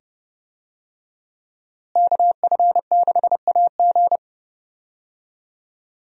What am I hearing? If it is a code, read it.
Morse code, heard as a keyed tone, about 30 wpm, 700 Hz. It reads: KF6AG
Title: USA Call Signs - 5 Characters - Repeated 2x - 30wpm